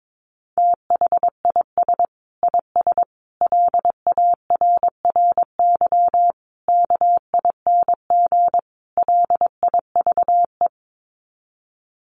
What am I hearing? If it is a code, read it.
Morse code, heard as a keyed tone, about 22 wpm, 700 Hz. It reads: THIS IS LARRY KING LIVE
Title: Sets of 5 Words Encore2 22wpm